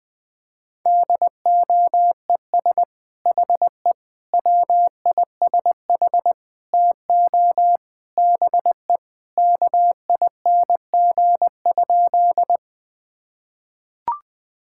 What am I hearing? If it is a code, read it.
Morse code, heard as a keyed tone, about 20 wpm, 700 Hz. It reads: DOES HE WISH TO BE KING?
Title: Sentences from Top 1600 Words 20wpm